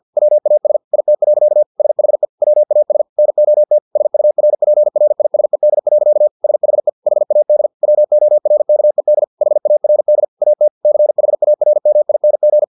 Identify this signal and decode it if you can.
Morse code, heard as a keyed tone, about 50 wpm, 600 Hz. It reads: WAS IT. SHE WAS NOT SURPRISED. SHE HAD WORKED HARD AT CHARMING
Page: Never 49wpm